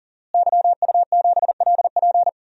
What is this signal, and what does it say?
Morse code, heard as a keyed tone, about 40 wpm, 700 Hz. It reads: YU7LP
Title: DXCC-100 Call Signs - 5 Characters - Repeated 2x - 40wpm